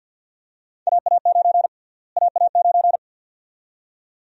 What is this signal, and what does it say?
Morse code, heard as a keyed tone, about 53 wpm, 700 Hz. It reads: AA9 AA9
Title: USA Call Sign Prefixes - Repeated 2x - 50wpm